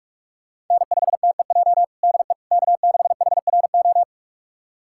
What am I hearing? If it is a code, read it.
Morse code, heard as a keyed tone, about 46 wpm, 700 Hz. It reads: N5TEJ DE K6HRO
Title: USA Callsign DE Callsign 45wpm